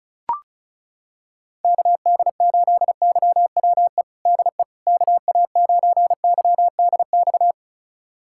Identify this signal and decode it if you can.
Morse code, heard as a keyed tone, about 36 wpm, 700 Hz. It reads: KD8YWE DE KA9YDX
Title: USA Callsign DE Callsign 35wpm